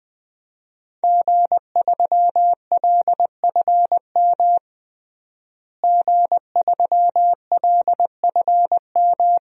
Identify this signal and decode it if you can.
Morse code, heard as a keyed tone, about 21 wpm, 700 Hz. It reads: G3LFM G3LFM
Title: DXCC-100 Call Signs - 5 Characters - Repeated 3x - 20wpm